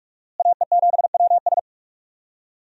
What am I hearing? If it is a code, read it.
Morse code, heard as a keyed tone, about 46 wpm, 700 Hz. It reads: AE7WS
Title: USA Call Signs - 5 Characters - Repeated 2x - 45wpm